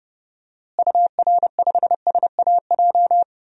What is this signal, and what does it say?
Morse code, heard as a keyed tone, about 32 wpm, 700 Hz. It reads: UR5SAJ
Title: DXCC-100 Call Signs - 6 Characters - Repeated 3x - 30wpm